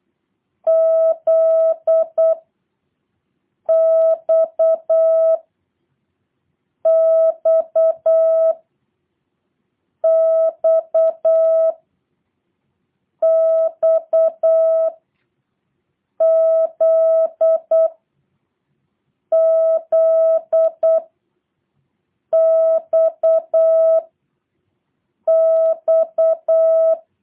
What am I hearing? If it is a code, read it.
Morse code, heard as a keyed tone, about 8 wpm, 650 Hz. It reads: Z X X X X Z Z X X